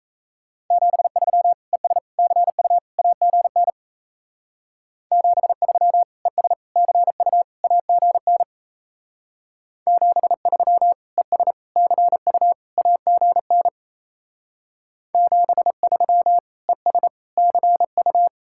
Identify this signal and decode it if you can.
Morse code, heard as a keyed tone, about 35 wpm, 700 Hz. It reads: 73 ES CU AGN 73 ES CU AGN 73 ES CU AGN 73 ES CU